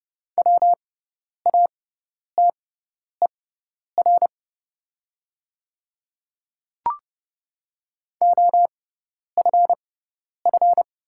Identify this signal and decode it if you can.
Morse code, heard as a keyed tone, about 30 wpm, spaced as for 10 wpm, 700 Hz. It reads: WATER OFF
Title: 6x Character Spacing - Top 300 Words - 30wpm